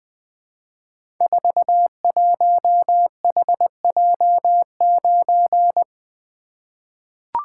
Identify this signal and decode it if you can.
Morse code, heard as a keyed tone, about 21 wpm, 700 Hz. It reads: 41HJ9
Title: Code Groups - 5 Letter:Numbers - 20wpm